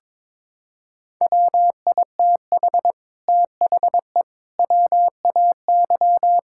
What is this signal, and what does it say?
Morse code, heard as a keyed tone, about 22 wpm, 700 Hz. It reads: WITH THE WAY
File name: practice / 22wpm / Sentences with 8-Word Phrases - 22wpm